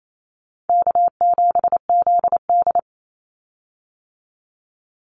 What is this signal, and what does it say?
Morse code, heard as a keyed tone, about 28 wpm, 700 Hz. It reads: K7ZD
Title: Easy Call Signs Repeated 2x - 28wpm